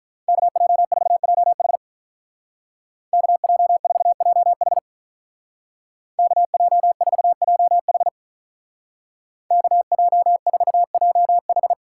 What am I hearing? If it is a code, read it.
Morse code, heard as a keyed tone, about 43 wpm, 700 Hz. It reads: KJ4JH KJ4JH KJ4JH KJ4JH
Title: Speed Racing - USA 2x2 Call Signs 35wpm